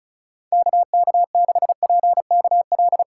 Morse code, heard as a keyed tone, about 34 wpm, 700 Hz. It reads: KK6PKL
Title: USA Call Signs - 6 Characters - 2x Character Spacing - 35wpm